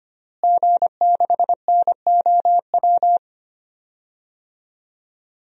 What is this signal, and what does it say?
Morse code, heard as a keyed tone, about 25 wpm, 700 Hz. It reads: G6NOW
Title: DXCC-100 Call Signs - 5 Characters - Repeated 3x - 25wpm